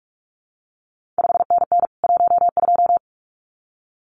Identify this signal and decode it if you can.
Morse code, heard as a keyed tone, about 46 wpm, 700 Hz. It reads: 5NN 12
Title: CQ WW DX Contest Exchange Encore - 45wpm